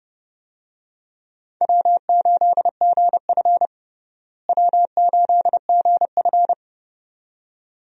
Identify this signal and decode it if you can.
Morse code, heard as a keyed tone, about 30 wpm, 700 Hz. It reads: W8GF W8GF
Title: USA Call Signs - 4 Characters - Repeated 2x - 30wpm